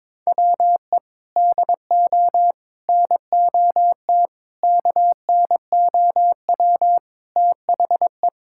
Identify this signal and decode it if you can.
Morse code, heard as a keyed tone, about 22 wpm, 700 Hz. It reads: WE DO NOT KNOW THE
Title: Sentences from Top 1000 Words - Review Part 6 - 22wpm